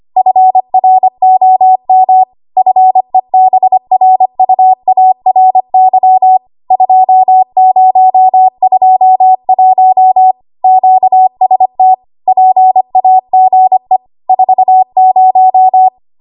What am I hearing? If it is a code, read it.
Morse code, heard as a keyed tone, about 25 wpm, 750 Hz. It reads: FROM FEBRUARY 2021 QST PAGE 40
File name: Archive / 25 WPM / 25WPM